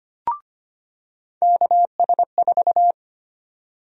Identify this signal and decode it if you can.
Morse code, heard as a keyed tone, about 25 wpm, 700 Hz. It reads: KS4
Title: USA Call Sign Prefixes - 25wpm